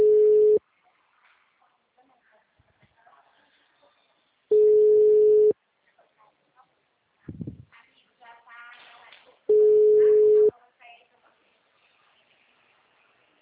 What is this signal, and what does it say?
Ringback tone (Europe, 425 Hz), with speech.
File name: out- -2006- - - .80713